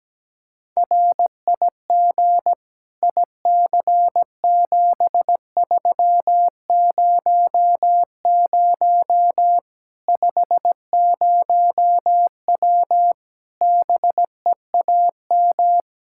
Morse code, heard as a keyed tone, about 17 wpm, 700 Hz. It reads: RIG IC7300 50W BEAM
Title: Practice QSO - Exchange 2 - Info - 17wpm